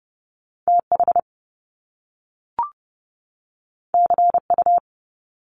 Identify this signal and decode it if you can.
Morse code, heard as a keyed tone, about 30 wpm, 700 Hz. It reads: TH CU
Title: Top Two Letter Combinations in Words 30wpm